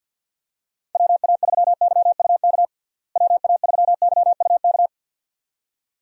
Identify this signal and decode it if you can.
Morse code, heard as a keyed tone, about 46 wpm, 700 Hz. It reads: WA3YUK WA3YUK
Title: USA Call Signs - 6 Characters - Repeated 2x - 50wpm